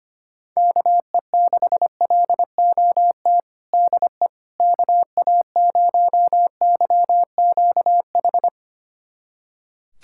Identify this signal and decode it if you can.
Morse code, heard as a keyed tone, about 25 wpm, 700 Hz. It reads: KE6LOT DE KA0YQH
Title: USA Callsign DE Callsign 25wpm